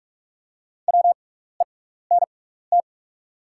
Morse code, heard as a keyed tone, about 44 wpm, spaced as for 15 wpm, 690 Hz. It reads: WENT
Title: 6x Character Spacing - Top 200 Words - 45wpm